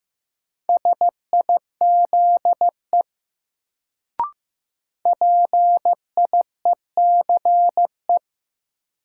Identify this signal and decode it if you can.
Morse code, heard as a keyed tone, about 15 wpm, 700 Hz. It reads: SIZE PIECE
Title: Sentences from Top 400 Words - New Word Drill - 15wpm